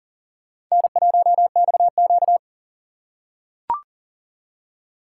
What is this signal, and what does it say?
Morse code, heard as a keyed tone, about 40 wpm, 700 Hz. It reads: N1XQ
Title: USA Call Signs - 4 Characters - 6x Character Spacing - 40wpm